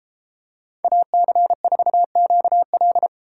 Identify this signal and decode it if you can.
Morse code, heard as a keyed tone, about 33 wpm, 700 Hz. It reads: AC4QL